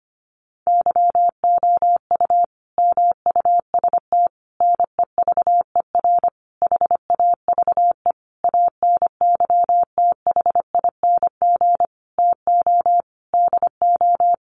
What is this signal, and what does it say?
Morse code, heard as a keyed tone, about 25 wpm, 700 Hz. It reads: YOU MUST NEVER HAVE ANYTHING TO DO
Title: Sentences from Top 1000 Words 25wpm